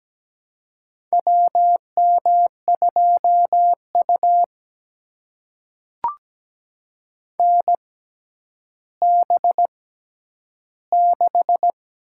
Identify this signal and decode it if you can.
Morse code, heard as a keyed tone, about 17 wpm, spaced as for 10 wpm, 700 Hz. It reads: WM2U NB6
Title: USA Call Signs - 4 Characters - 6x Character Spacing - 17wpm